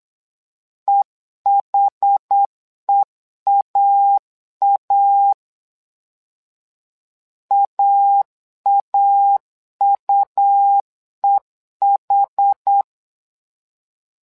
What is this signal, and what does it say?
Morse code, heard as a keyed tone, about 8 wpm, 800 Hz. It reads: EHEAA AAUEH
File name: LEZIONI / 2 UHDN / 2 UH 8wpm 2129